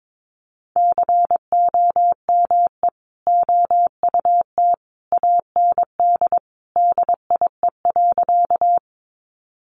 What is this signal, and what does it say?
Morse code, heard as a keyed tone, about 22 wpm, 700 Hz. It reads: COME OUT AND DIE.
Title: Sentences from Top 1300 Words 22wpm